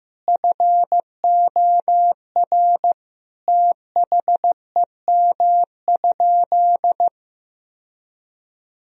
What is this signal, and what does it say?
Morse code, heard as a keyed tone, about 15 wpm, 700 Hz. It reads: FOR THEM?
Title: Sentences from Top 1000 Words - Review Part 1 - 15wpm